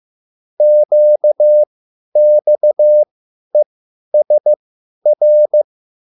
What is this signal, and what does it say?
Morse code, heard as a keyed tone, about 15 wpm, spaced as for 10 wpm, 600 Hz. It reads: QXESR